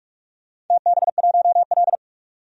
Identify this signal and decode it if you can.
Morse code, heard as a keyed tone, about 44 wpm, 700 Hz. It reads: TB1L
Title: DXCC-100 Call Signs - 4 Characters - 45wpm